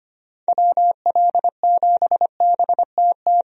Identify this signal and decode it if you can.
Morse code, heard as a keyed tone, about 25 wpm, 700 Hz. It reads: WL7BTT
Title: USA Call Signs - 6 Characters - 25wpm